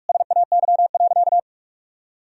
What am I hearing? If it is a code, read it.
Morse code, heard as a keyed tone, about 41 wpm, 700 Hz. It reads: SAY.
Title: Sentences from Top 1000 Words - Review Part 2 - 45wpm